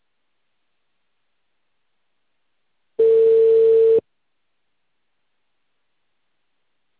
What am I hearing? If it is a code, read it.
Ringback tone (China, 450 Hz).
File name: out- -2044- - - .71460